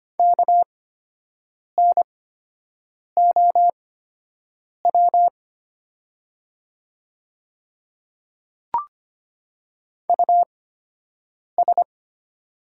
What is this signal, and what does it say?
Morse code, heard as a keyed tone, about 25 wpm, spaced as for 7 wpm, 700 Hz. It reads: KNOW US
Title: 8x Character Spacing - Top 200 Words - 25wpm